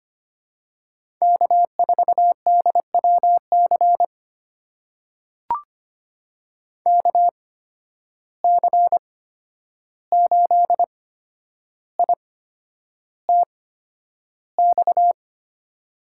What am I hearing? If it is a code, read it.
Morse code, heard as a keyed tone, about 25 wpm, spaced as for 10 wpm, 700 Hz. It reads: K4DWC KC8ITX
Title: USA Call Signs - 8x Character Spacing - 25wpm